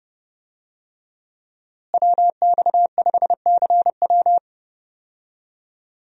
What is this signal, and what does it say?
Morse code, heard as a keyed tone, about 30 wpm, 700 Hz. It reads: WX5CW
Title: USA Call Signs - 5 Characters - Repeated 2x - 30wpm